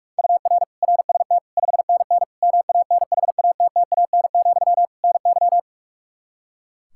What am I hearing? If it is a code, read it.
Morse code, heard as a keyed tone, about 46 wpm, 700 Hz. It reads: UR RST 5NN MANHATTAN, NY
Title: Practice QSO - Exchange 1 - Beginning - 45wpm